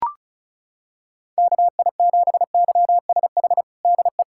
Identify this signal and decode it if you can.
Morse code, heard as a keyed tone, about 36 wpm, 700 Hz. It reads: KI7YSH DE
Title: USA Callsign DE Callsign 35wpm